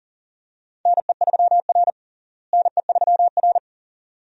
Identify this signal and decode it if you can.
Morse code, heard as a keyed tone, about 40 wpm, 700 Hz. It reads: NE3R NE3R
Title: USA Call Signs - 4 Characters - Repeated 2x - 40wpm